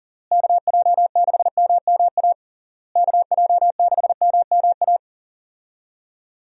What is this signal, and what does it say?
Morse code, heard as a keyed tone, about 40 wpm, 700 Hz. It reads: KJ6MMA KJ6MMA
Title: USA Call Signs - 6 Characters - Repeated 3x - 40wpm